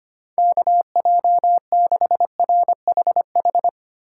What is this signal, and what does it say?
Morse code, heard as a keyed tone, about 25 wpm, 700 Hz. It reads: KJ6RHH
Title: USA Call Signs - 6 Characters - 3x Character Spacing - 25wpm